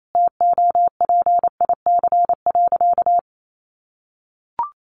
Morse code, heard as a keyed tone, about 28 wpm, 700 Hz. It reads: TOPIC.
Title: Sentences from Top 1200 Words 28wpm